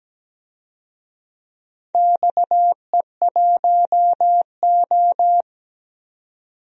Morse code, heard as a keyed tone, about 17 wpm, 700 Hz. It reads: XE1O
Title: DXCC-100 Call Signs - 4 Characters - 17wpm